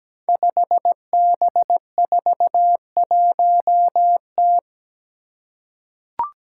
Morse code, heard as a keyed tone, about 17 wpm, 700 Hz. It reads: 5B41T
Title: Five Challenging Letter-Number Combinations 17wpm